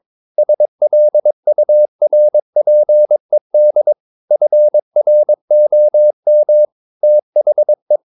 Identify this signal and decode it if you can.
Morse code, heard as a keyed tone, about 22 wpm, 600 Hz. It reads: SLURPED FROM THE
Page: Romanen Never af Ken Follet på engelsk.